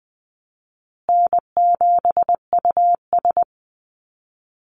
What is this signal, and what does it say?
Morse code, heard as a keyed tone, about 20 wpm, 700 Hz. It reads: N7US
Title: Call Signs Repeated 3x - 20wpm